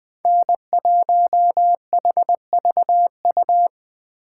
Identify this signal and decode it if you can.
Morse code, heard as a keyed tone, about 21 wpm, 700 Hz. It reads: N1HVU
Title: Speed Racing - USA 1x3 Call Signs 17wpm